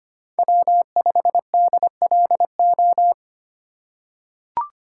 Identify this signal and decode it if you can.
Morse code, heard as a keyed tone, about 25 wpm, 700 Hz. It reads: W5DLO
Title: USA Call Signs - 5 Characters - 3x Character Spacing - 25wpm